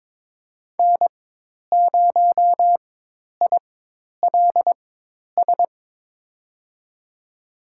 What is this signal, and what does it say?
Morse code, heard as a keyed tone, about 22 wpm, spaced as for 10 wpm, 700 Hz. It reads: N0ILS
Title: USA Call Signs - 5 Characters - 4x Character Spacing - 22wpm